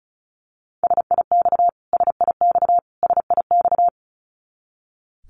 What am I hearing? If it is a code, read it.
Morse code, heard as a keyed tone, about 34 wpm, 700 Hz. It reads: SIX SIX SIX
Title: Top 500 Words Repeated 35wpm